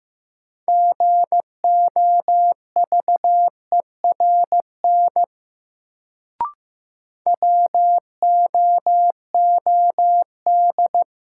Morse code, heard as a keyed tone, about 15 wpm, 700 Hz. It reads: GOVERN WOOD
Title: Sentences from Top 1000 Words - Review Part 2 - New Word Drill - 15wpm